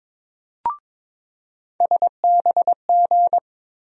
Morse code, heard as a keyed tone, about 22 wpm, 700 Hz. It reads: SBG
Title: USA Call Sign Postfixes - 3 Letters 22wpm